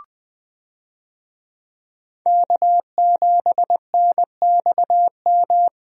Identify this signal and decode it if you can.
Morse code, heard as a keyed tone, about 20 wpm, 700 Hz. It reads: K7NXM
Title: USA Call Signs - 5 Characters - Repeated 2x - 20wpm